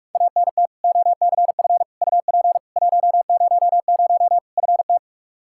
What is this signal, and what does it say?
Morse code, heard as a keyed tone, about 46 wpm, 700 Hz. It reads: ANT OCF UP 100 FT
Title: K1USN SST QSO Practice 45wpm